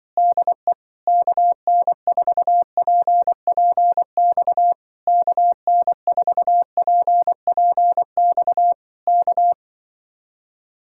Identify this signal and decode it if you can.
Morse code, heard as a keyed tone, about 24 wpm, 700 Hz. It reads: DE KN4PPX KN4PPX K
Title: Speed-Racing Common QSO Phrases Encore - 20wpm